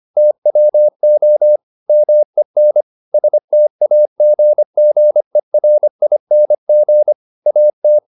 Morse code, heard as a keyed tone, about 25 wpm, 600 Hz. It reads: TWO MEN STAGGERING AT
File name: the old man and the sea part 1 25wpm